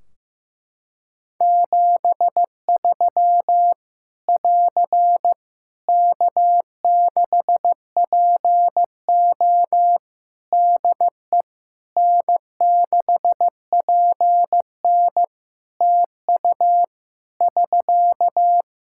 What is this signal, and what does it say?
Morse code, heard as a keyed tone, about 15 wpm, 700 Hz. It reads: 73 <AR> K6PO DE N6PN TU <SK>
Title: K1USN SST QSO Practice 15wpm